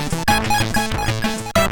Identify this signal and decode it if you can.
There is no code.